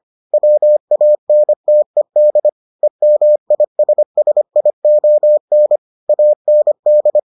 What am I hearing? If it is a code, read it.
Morse code, heard as a keyed tone, about 25 wpm, 600 Hz. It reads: WANTED EMISSION AND